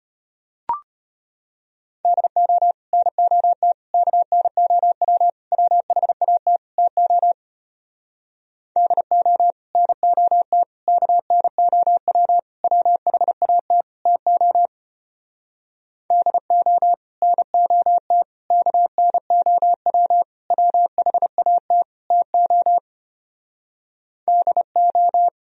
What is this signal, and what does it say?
Morse code, heard as a keyed tone, about 33 wpm, 700 Hz. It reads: DO NOT KNOW WHAT TO DO NOT KNOW WHAT TO DO NOT KNOW WHAT TO DO